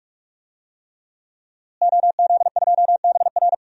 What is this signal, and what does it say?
Morse code, heard as a keyed tone, about 41 wpm, 700 Hz. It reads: OZ2BR
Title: DXCC-100 Call Signs - 5 Characters - Repeated 3x - 45wpm